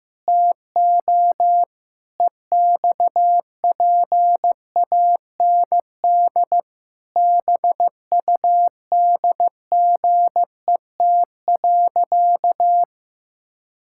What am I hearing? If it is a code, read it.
Morse code, heard as a keyed tone, about 15 wpm, 700 Hz. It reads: TO EXPAND BUDGET.
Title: Pangram Challenge v2 - 15wpm